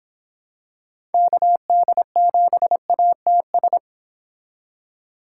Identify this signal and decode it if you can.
Morse code, heard as a keyed tone, about 26 wpm, 700 Hz. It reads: KD7ATS